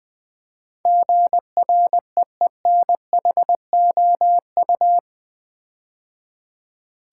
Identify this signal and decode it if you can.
Morse code, heard as a keyed tone, about 20 wpm, 700 Hz. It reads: GREENHOU
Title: Build A Word - 4 Letters At A Time - 10-Letter Words - 20wpm